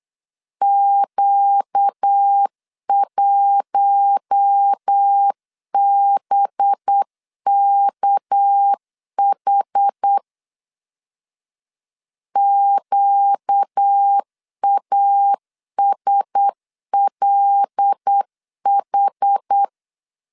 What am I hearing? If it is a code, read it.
Morse code, heard as a keyed tone, about 8 wpm, 800 Hz. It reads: Q1BKH QASLH
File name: LEZIONI / 6 PKX0 / 6 PK 8wpm 3003